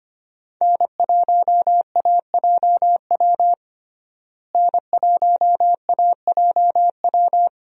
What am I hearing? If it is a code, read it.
Morse code, heard as a keyed tone, about 25 wpm, 700 Hz. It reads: N1AJW N1AJW
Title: USA Call Signs - 5 Characters - Repeated 2x - 25wpm